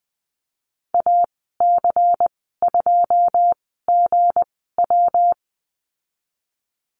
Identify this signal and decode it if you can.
Morse code, heard as a keyed tone, about 20 wpm, spaced as for 14 wpm, 700 Hz. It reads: AC2GW
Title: 193 Lesson 31g - Letter v - Call Signs 20wpm